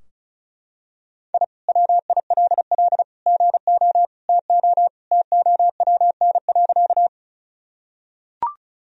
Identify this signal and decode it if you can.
Morse code, heard as a keyed tone, about 34 wpm, 700 Hz. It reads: I WILL GO TO TOWN.
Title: Sentences from Top 1000 Words - Review Part 2 - 35wpm